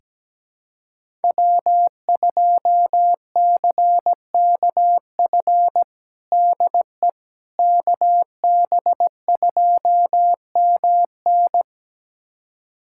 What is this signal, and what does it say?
Morse code, heard as a keyed tone, about 17 wpm, 700 Hz. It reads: W2CKF DE KB2MN
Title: USA Callsign DE Callsign - 5 Characters - 17wpm